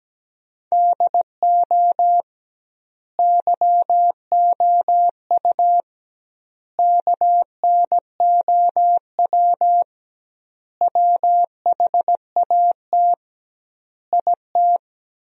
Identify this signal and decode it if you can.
Morse code, heard as a keyed tone, about 17 wpm, 700 Hz. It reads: DO YOU KNOW WHAT IT
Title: 2x Word-Spacing - Sentences from Top 100 Words - 17wpm